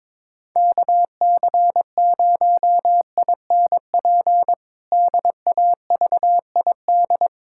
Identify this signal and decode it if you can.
Morse code, heard as a keyed tone, about 22 wpm, 700 Hz. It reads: KC0INP DAVID
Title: K1USN SST QSO Practice - 22wpm